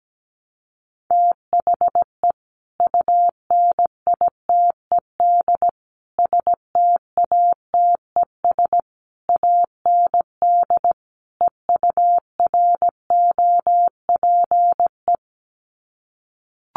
Morse code, heard as a keyed tone, about 17 wpm, 700 Hz. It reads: THE UNITED STATES AND EUROPE
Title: Sets of 5 Words Encore 17wpm